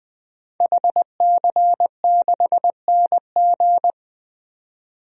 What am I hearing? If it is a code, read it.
Morse code, heard as a keyed tone, about 20 wpm, 700 Hz. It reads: HC6NG
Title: DXCC-100 Call Signs - 5 Characters - 20wpm